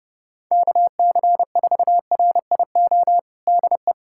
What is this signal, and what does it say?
Morse code, heard as a keyed tone, about 32 wpm, 700 Hz. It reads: KC4RIO DE
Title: USA Callsign DE Callsign 30wpm